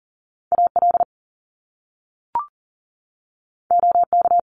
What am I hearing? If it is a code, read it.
Morse code, heard as a keyed tone, about 40 wpm, 700 Hz. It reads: AL OK
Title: Top Two Letter Combinations in Words 40wpm